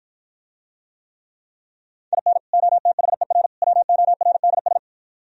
Morse code, heard as a keyed tone, about 52 wpm, 690 Hz. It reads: IN OTHER WORDS